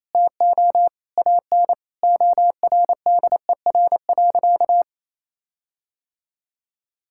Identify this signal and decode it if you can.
Morse code, heard as a keyed tone, about 28 wpm, 700 Hz. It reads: TO AN ORDER.
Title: Sentences from Top 1600 Words 28wpm